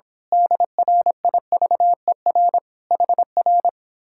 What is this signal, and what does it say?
Morse code, heard as a keyed tone, about 26 wpm, 700 Hz. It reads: DRIVER HR
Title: Speed-Racing Common QSO Phrases - 22wpm